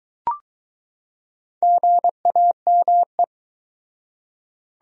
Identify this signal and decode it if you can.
Morse code, heard as a keyed tone, about 22 wpm, 700 Hz. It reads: GAME